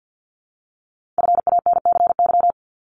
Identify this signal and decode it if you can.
Morse code, heard as a keyed tone, about 50 wpm, 700 Hz. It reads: FANCY
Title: Multi-letter Phonemes 50wpm